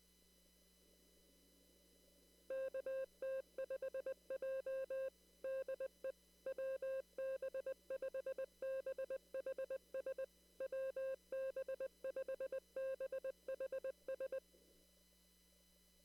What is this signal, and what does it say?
Morse code, heard as a keyed tone, about 20 wpm, 530 Hz. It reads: KT5J DE WB5BHS WB5BHS
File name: CW / kt5j wpx cw-op-k5pi